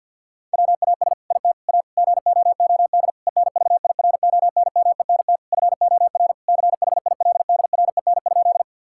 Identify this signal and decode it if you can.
Morse code, heard as a keyed tone, about 50 wpm, 700 Hz. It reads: WAS IT A GOOD ENVIRONMENT FOR CHILDREN?
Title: Sentences from Top 1000 Words - Review Part 5 - 50wpm